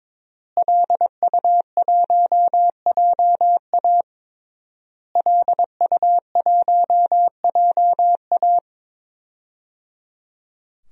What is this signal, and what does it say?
Morse code, heard as a keyed tone, about 22 wpm, 700 Hz. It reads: LU1JA LU1JA
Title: DXCC-100 Call Signs - 5 Characters - Repeated 3x - 22wpm